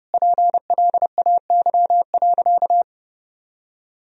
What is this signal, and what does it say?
Morse code, heard as a keyed tone, about 30 wpm, 700 Hz. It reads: PLAY.
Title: Sentences from Top 1000 Words - Review Part 4 - 30wpm